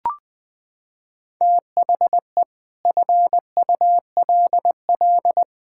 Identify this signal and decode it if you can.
Morse code, heard as a keyed tone, about 20 wpm, 700 Hz. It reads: THE FULL
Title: Sentences with 7-Word Phrases - 20wpm